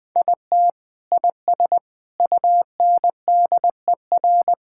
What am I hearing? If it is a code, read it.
Morse code, heard as a keyed tone, about 21 wpm, 700 Hz. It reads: IT IS UNDER
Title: Sentences from Top 1000 Words - Review Part 6 - 20wpm